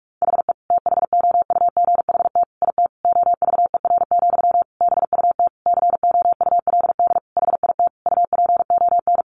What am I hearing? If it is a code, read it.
Morse code, heard as a keyed tone, about 44 wpm, 700 Hz. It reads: HE THOUGHT IT OVER, BUT COULD HIT UPON
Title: Sentences from Top 1200 Words 45wpm